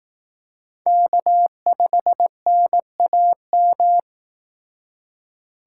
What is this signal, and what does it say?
Morse code, heard as a keyed tone, about 18 wpm, 700 Hz. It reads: K5NAM
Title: Speed Racing - USA 1x3 Call Signs 15wpm